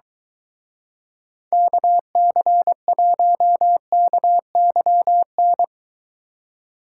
Morse code, heard as a keyed tone, about 23 wpm, 700 Hz. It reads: KC1KYN